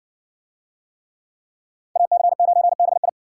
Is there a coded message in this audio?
Yes: Morse code from 2.0 to 3.1 s, heard as a keyed tone, about 60 wpm, 700 Hz. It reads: AC9BI